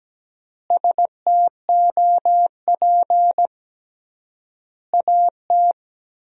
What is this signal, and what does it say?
Morse code, heard as a keyed tone, about 17 wpm, 700 Hz. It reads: STOP AT
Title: 3x Word-Spacing - Sentences from Top 400 Words - 17wpm